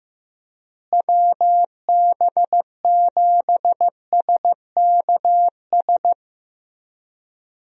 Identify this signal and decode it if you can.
Morse code, heard as a keyed tone, about 15 wpm, 700 Hz. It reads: WB7SKS
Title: USA Call Signs - 6 Characters - Repeated 3x - 15wpm